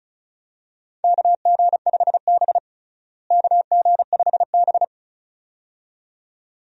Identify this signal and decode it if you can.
Morse code, heard as a keyed tone, about 36 wpm, 700 Hz. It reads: KG5B KG5B
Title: USA Call Signs - 4 Characters - Repeated 2x - 35wpm